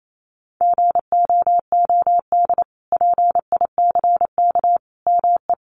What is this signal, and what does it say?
Morse code, heard as a keyed tone, about 28 wpm, 700 Hz. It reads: GOOD PICK ME
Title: Sentences from Top 1000 Words 28wpm